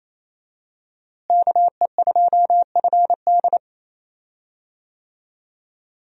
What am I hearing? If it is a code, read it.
Morse code, heard as a keyed tone, about 28 wpm, 700 Hz. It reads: KE2FD
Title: USA Call Signs - 5 Characters - Repeated 2x - 28wpm